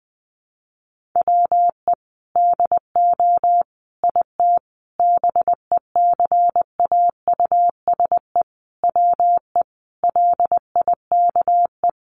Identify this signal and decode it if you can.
Morse code, heard as a keyed tone, about 20 wpm, 700 Hz. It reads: WE DO IT BECAUSE WE LIKE
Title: Sentences from Top 1400 Words 20wpm